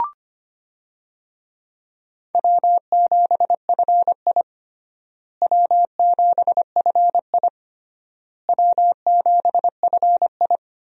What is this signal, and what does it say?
Morse code, heard as a keyed tone, about 25 wpm, 700 Hz. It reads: W7FI W7FI W7FI
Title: USA Call Signs - 4 Characters - Repeated 3x - 25wpm